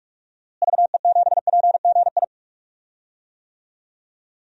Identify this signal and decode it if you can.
Morse code, heard as a keyed tone, about 43 wpm, 690 Hz. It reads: VE7PGI
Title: DXCC-100 Call Signs - 6 Characters - Repeated 2x - 45wpm